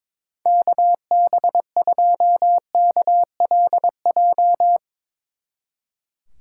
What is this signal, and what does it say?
Morse code, heard as a keyed tone, about 22 wpm, 700 Hz. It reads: KB2KLJ
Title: Speed Racing - USA 2x3 Call Signs 22wpm